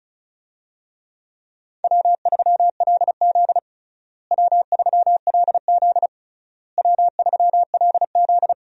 Morse code, heard as a keyed tone, about 36 wpm, 700 Hz. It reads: W3LZ W3LZ W3LZ
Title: USA Call Signs - 4 Characters - Repeated 3x - 35wpm